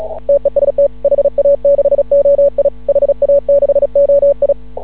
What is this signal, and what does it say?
Morse code, heard as a keyed tone, about 36 wpm, 560 Hz. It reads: TEST HA6OI HA6OI
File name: CW / ha6oi 2